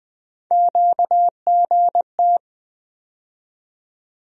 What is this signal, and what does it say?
Morse code, heard as a keyed tone, about 20 wpm, 700 Hz. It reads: QGT
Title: USA Call Sign Postfixes - 3 Letters 20wpm